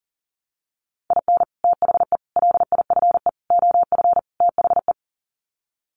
Morse code, heard as a keyed tone, about 40 wpm, 700 Hz. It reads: IN THE LIFE OF THE
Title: Sets of 5 Words Encore2 40wpm